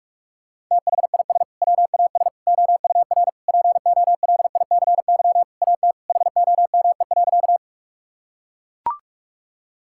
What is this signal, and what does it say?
Morse code, heard as a keyed tone, about 42 wpm, 690 Hz. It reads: THIS WAS OUR POLICY AT HOME.
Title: Sentences from Top 1000 Words - Review Part 5 - 45wpm